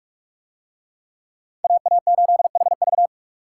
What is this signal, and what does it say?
Morse code, heard as a keyed tone, about 45 wpm, 700 Hz. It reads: AA8HV